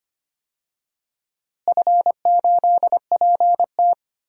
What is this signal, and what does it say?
Morse code, heard as a keyed tone, about 25 wpm, 700 Hz. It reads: F8PT
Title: DXCC-100 Call Signs - 4 Characters - Repeated 3x - 25wpm